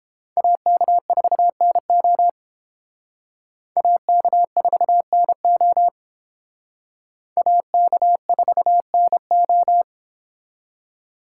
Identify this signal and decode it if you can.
Morse code, heard as a keyed tone, about 30 wpm, 700 Hz. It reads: AK4NO AK4NO AK4NO